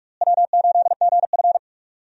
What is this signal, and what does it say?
Morse code, heard as a keyed tone, about 42 wpm, 700 Hz. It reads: W8GF
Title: USA Call Signs - 4 Characters - Repeated 2x - 45wpm